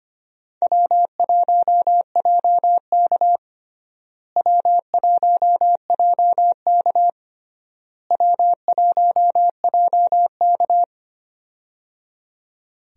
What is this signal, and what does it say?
Morse code, heard as a keyed tone, about 25 wpm, 700 Hz. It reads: W1JK W1JK W1JK
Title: USA Call Signs - 4 Characters - Repeated 3x - 25wpm